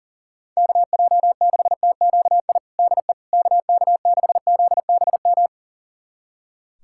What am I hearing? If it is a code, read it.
Morse code, heard as a keyed tone, about 37 wpm, 700 Hz. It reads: KJ6TQI DE KK6ZBM
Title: USA Callsign DE Callsign 40wpm